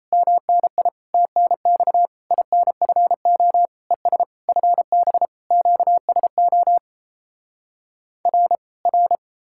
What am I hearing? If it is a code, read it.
Morse code, heard as a keyed tone, about 37 wpm, 700 Hz. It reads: MNI TNX INFO ES FB QSO R R